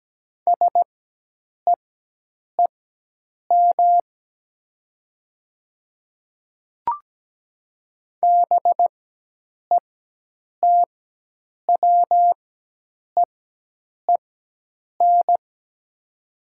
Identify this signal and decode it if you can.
Morse code, heard as a keyed tone, about 17 wpm, spaced as for 8 wpm, 700 Hz. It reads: SEEM BETWEEN
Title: 4x Character Spacing - Top 500 Words - 17wpm